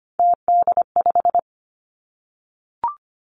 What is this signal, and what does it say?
Morse code, heard as a keyed tone, about 25 wpm, 700 Hz. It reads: TD5
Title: Three Challenging Letter-Number Combinations 25wpm